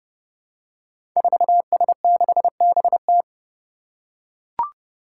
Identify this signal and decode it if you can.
Morse code, heard as a keyed tone, about 28 wpm, 700 Hz. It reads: 4S6BT
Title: Five Challenging Letter-Number Combinations 30wpm